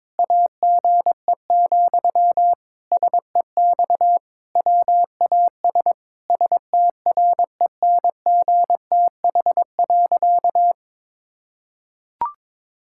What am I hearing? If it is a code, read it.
Morse code, heard as a keyed tone, about 22 wpm, 700 Hz. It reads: AGE, SEX WAS STRENGTH.
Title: Sentences from Top 1600 Words 22wpm